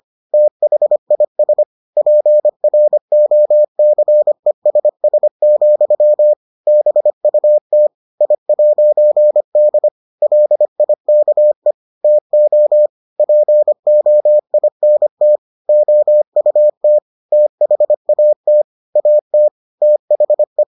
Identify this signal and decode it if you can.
Morse code, heard as a keyed tone, about 25 wpm, 600 Hz. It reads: THIS PROCESS, BUT I'D LIKE TO POINT OUT THAT AT THE